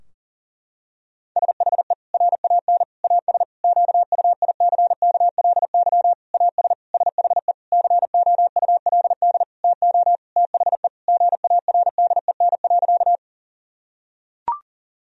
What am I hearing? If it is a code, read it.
Morse code, heard as a keyed tone, about 40 wpm, 700 Hz. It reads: SHE RAN AS QUICKLY AS SHE COULD TO THE GARDEN.
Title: Sentences from Top 1000 Words - Review Part 4 - 40wpm